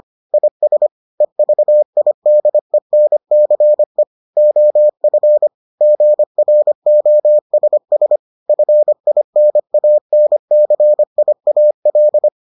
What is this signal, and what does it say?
Morse code, heard as a keyed tone, about 25 wpm, 600 Hz. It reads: IS EVIDENCE OF GROSS FINANCIAL